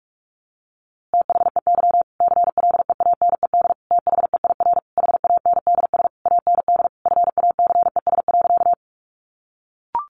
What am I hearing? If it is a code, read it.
Morse code, heard as a keyed tone, about 44 wpm, 700 Hz. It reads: THEY CLEANED THEIR HANDS AND FACES.
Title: A Sound Of Thunder 45wpm